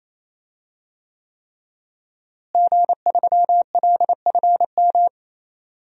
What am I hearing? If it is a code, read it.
Morse code, heard as a keyed tone, about 28 wpm, 700 Hz. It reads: G3LFM
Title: DXCC-100 Call Signs - 5 Characters - Repeated 3x - 28wpm